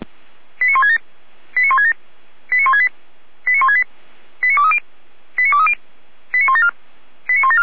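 Five-tone selective call (ZVEI), heard as quick runs of five tones, tones of about 70 ms. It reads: 89167 89167 89168 89168 89230 89230 89174 89174
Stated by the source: ZVEI five-tone selective call received on an SDR.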